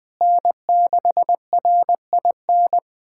Morse code, heard as a keyed tone, about 20 wpm, 700 Hz. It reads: N6RIN
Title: USA Callsign DE Callsign - 5 Characters - 20wpm